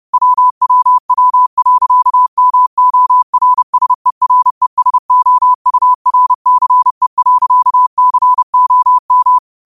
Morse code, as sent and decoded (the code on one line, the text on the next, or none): .-- .-- .-- .-.-.- -- --- .-. ... .-. ... --- ..- .-. -.-. .-.-.- -.-. --- --
WWW.MORSERESOURCE.COM